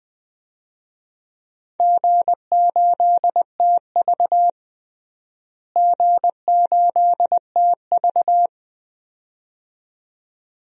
Morse code, heard as a keyed tone, about 20 wpm, 700 Hz. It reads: G8TV G8TV
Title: DXCC-100 Call Signs - 4 Characters - Repeated 2x - 20wpm